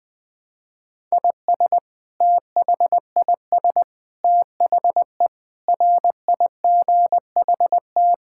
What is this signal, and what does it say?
Morse code, heard as a keyed tone, about 20 wpm, 700 Hz. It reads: IS THIS THE RIGHT
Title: Sentences from Top 1000 Words - Review Part 6 - 20wpm